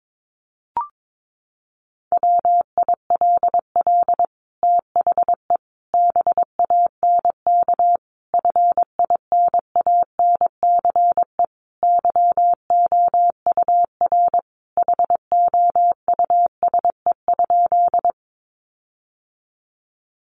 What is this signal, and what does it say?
Morse code, heard as a keyed tone, about 22 wpm, 700 Hz. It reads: WILL THE BANK FINANCE YOUR HOUSE?
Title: Sentences from Top 1500 Words 22wpm